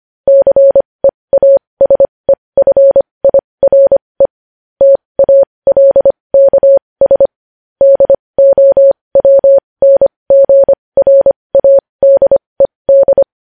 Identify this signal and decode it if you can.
Morse code, heard as a keyed tone, about 25 wpm, spaced as for 20 wpm, 550 Hz. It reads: CEASEFIRE TALKS DOWNGRADED
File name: podcast / News headlines-1x-25x20